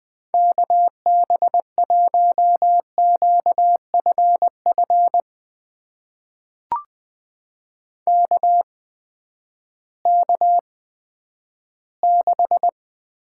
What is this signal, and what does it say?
Morse code, heard as a keyed tone, about 20 wpm, spaced as for 11 wpm, 700 Hz. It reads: KB1QFF KK6
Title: USA Call Signs - 6 Characters - 8x Character Spacing - 20wpm